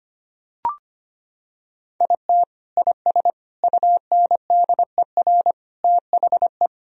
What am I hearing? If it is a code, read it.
Morse code, heard as a keyed tone, about 25 wpm, 700 Hz. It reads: IT IS UNDER THE
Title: Sentences from Top 1000 Words - Review Part 1 - 25wpm